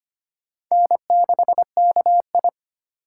Morse code, heard as a keyed tone, about 25 wpm, 700 Hz. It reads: N6KI
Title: USA Call Signs - 3x Character Spacing - 25wpm